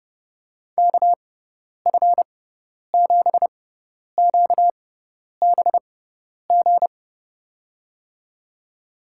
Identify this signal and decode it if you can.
Morse code, heard as a keyed tone, about 30 wpm, spaced as for 10 wpm, 700 Hz. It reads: KF7QBG
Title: USA Call Signs - 6 Characters - 6x Character Spacing - 30wpm